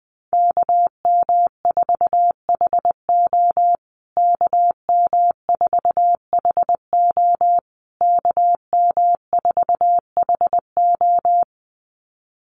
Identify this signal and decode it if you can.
Morse code, heard as a keyed tone, about 19 wpm, 700 Hz. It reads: KM4HO KM4HO KM4HO
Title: USA Call Signs Repeated 3x 20wpm